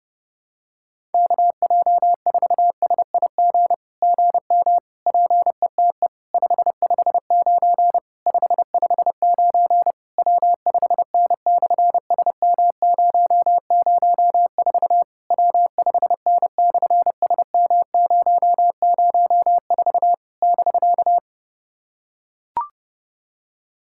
Morse code, heard as a keyed tone, about 28 wpm, 700 Hz. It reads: KJ4SIG GM PETE 559 559 W5N/SM004 W5N/SM004 <BK>
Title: SOTA QSO Practice 30wpm